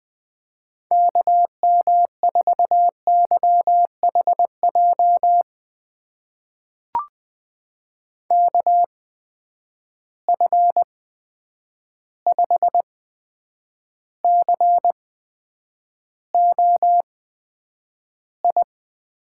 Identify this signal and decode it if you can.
Morse code, heard as a keyed tone, about 20 wpm, spaced as for 9 wpm, 700 Hz. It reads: KM4YHJ KF5COI
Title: USA Call Signs - 6 Characters - 8x Character Spacing - 20wpm